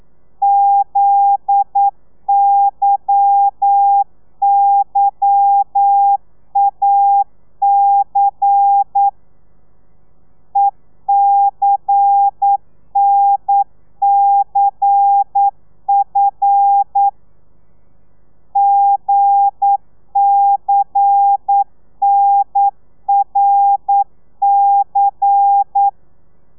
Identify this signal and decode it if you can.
Morse code, heard as a keyed tone, about 9 wpm, 800 Hz. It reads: ZYYAC ECNCF GCNRC